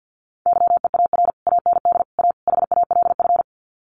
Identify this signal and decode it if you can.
Morse code, heard as a keyed tone, about 50 wpm, 700 Hz. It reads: YEAR AND A HALF
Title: Sets of 4 Words 50wpm